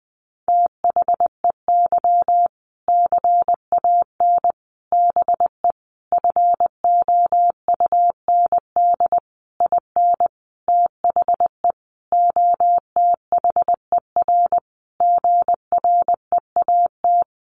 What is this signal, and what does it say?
Morse code, heard as a keyed tone, about 19 wpm, 700 Hz. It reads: THEY CAN BE FOUND IN THE OTHER GREAT
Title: Sentences from Top 1500 Words 20wpm